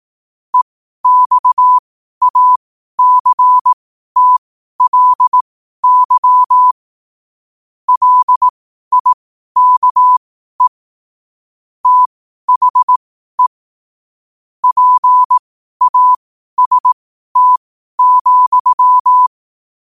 Morse code, heard as a keyed tone, about 18 wpm, spaced as for 12 wpm, 1000 Hz. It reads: EXACTLY LIKE THE PAST,
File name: qotdfiles / 12wpm